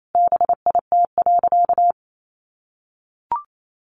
Morse code, heard as a keyed tone, about 28 wpm, 700 Hz. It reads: BIT.
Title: Sentences from Top 900 Words 28wpm